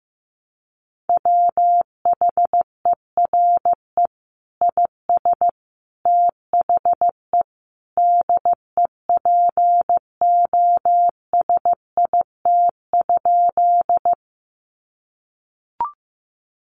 Morse code, heard as a keyed tone, about 15 wpm, 700 Hz. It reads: WHERE IS THE DEPOSIT?
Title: Sentences from Top 1300 Words 15wpm